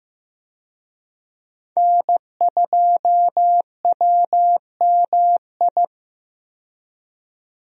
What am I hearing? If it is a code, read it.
Morse code, heard as a keyed tone, about 15 wpm, 700 Hz. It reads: N2WMI
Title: USA Call Signs - 5 Characters - 15wpm